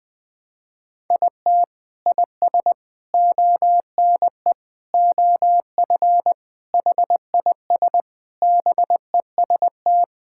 Morse code, heard as a keyed tone, about 20 wpm, 700 Hz. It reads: IT IS ONE OF HIS BEST
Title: Sentences with 3-Word Phrases - 20wpm